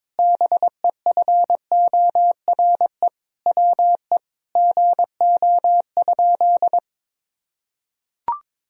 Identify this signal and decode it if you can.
Morse code, heard as a keyed tone, about 22 wpm, 700 Hz. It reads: BEFORE WE GO?
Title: Sentences from Top 1000 Words - Review Part 2 - 22wpm